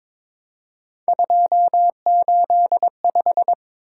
Morse code, heard as a keyed tone, about 22 wpm, 700 Hz. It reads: 285
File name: practice / 22wpm / Three Letter-Number Combinations - 22wpm